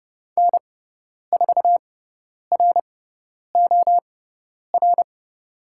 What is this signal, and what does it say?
Morse code, heard as a keyed tone, about 29 wpm, spaced as for 10 wpm, 700 Hz. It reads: N4ROR
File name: Lesson 17g - Letter u - Call Signs 30wpm